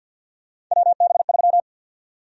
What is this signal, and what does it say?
Morse code, heard as a keyed tone, about 49 wpm, 700 Hz. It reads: WB3
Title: USA Call Sign Prefixes - 50wpm